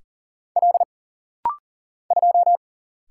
Morse code, heard as a keyed tone, about 40 wpm, 700 Hz. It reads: L 2
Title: Single Letter-Number Alt - Rapid-Fire 40wpm